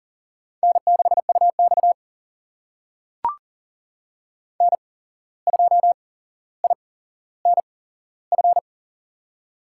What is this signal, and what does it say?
Morse code, heard as a keyed tone, about 40 wpm, spaced as for 16 wpm, 700 Hz. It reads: N6UX N2INF
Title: USA Call Signs - 8x Character Spacing - 40wpm